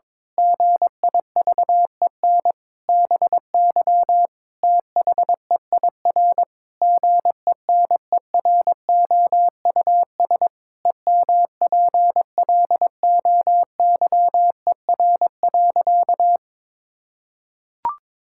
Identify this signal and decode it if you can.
Morse code, heard as a keyed tone, about 22 wpm, 700 Hz. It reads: GIVEN BY THEIR GENEROUS EMPLOYER.
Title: Pangram Challenge v2 - 22wpm